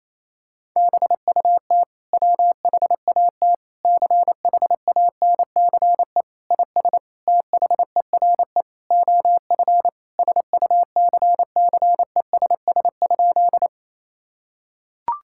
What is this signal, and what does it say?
Morse code, heard as a keyed tone, about 28 wpm, 690 Hz. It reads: BUT WHAT CHANCE IS THERE OF SUCCESS?
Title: Sentences from Top 1000 Words - Review Part 6 - 28wpm